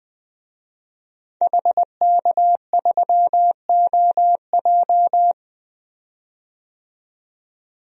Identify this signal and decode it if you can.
Morse code, heard as a keyed tone, about 21 wpm, 700 Hz. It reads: HK3OJ
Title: DXCC-100 Call Signs - 5 Characters - 20wpm